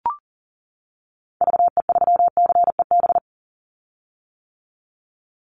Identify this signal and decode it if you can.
Morse code, heard as a keyed tone, about 40 wpm, 700 Hz. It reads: VE3CEB
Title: Call Signs - 40wpm